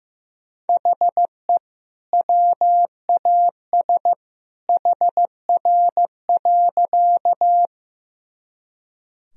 Morse code, heard as a keyed tone, about 15 wpm, 700 Hz. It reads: HE WAS HR.
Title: Sentences from Top 200 Words and Abbreviations 15wpm